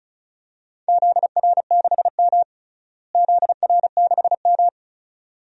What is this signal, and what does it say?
Morse code, heard as a keyed tone, about 36 wpm, 700 Hz. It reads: ZR6M ZR6M
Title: DXCC-100 Call Signs - 4 Characters - Repeated 2x - 35wpm